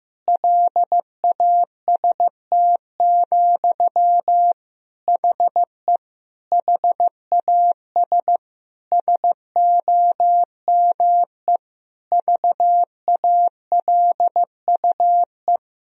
Morse code, heard as a keyed tone, about 16 wpm, 700 Hz. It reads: LAST, HE HAS SOME VALUE
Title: Sentences from Top 1000 Words - Review Part 3 - 15wpm